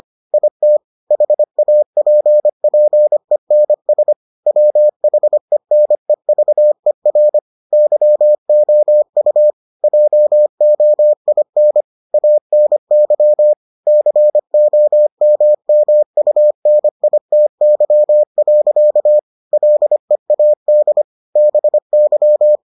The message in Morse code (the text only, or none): IT HAPPENS WHENEVER YOU JOIN ANY COMMUNITY. LEAD BY